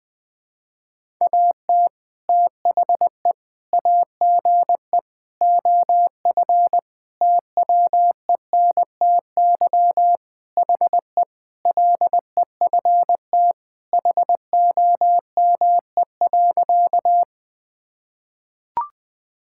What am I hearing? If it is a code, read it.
Morse code, heard as a keyed tone, about 20 wpm, 700 Hz. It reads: AT THE AGE OF TWENTY HE LEFT HOME.
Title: Sentences with 4-Word Phrases - Spoken Beginning - 20wpm